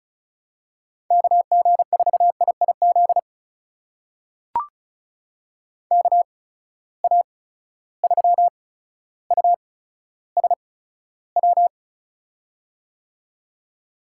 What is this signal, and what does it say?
Morse code, heard as a keyed tone, about 36 wpm, spaced as for 15 wpm, 700 Hz. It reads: KG4IIZ KA3USW
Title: USA Call Signs - 6 Characters - 8x Character Spacing - 35wpm